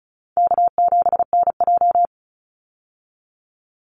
Morse code, heard as a keyed tone, about 34 wpm, 700 Hz. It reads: K7NJ
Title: Call Signs Repeated 3x - 35wpm